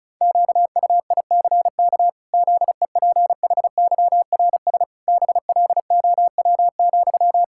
Morse code, heard as a keyed tone, about 34 wpm, 700 Hz. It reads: QUICK ZEPHYRS BLOW,
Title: Pangram Challenge v2 - 35wpm